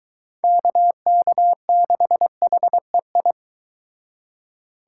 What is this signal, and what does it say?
Morse code, heard as a keyed tone, about 23 wpm, 700 Hz. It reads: KK6HEI